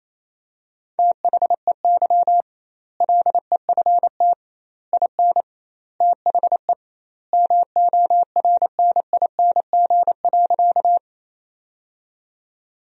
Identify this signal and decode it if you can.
Morse code, heard as a keyed tone, about 28 wpm, 700 Hz. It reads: THEY LEFT IN THE MORNING.
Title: 2x Word-Spacing - Sentences from Top 400 Words - 28wpm